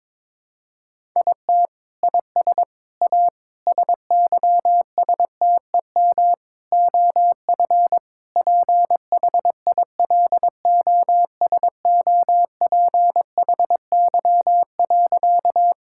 Morse code, heard as a keyed tone, about 22 wpm, 700 Hz. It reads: IT IS A SYSTEM OF PHILOSOPHY.
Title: Sentences from Top 1000 Words - Review Part 6 - 22wpm